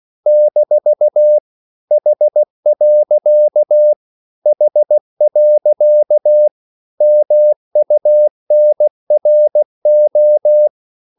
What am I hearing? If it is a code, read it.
Morse code, heard as a keyed tone, about 16 wpm, 600 Hz. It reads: - H. H. MUNRO
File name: CW / Practice / EnglishQuotations0216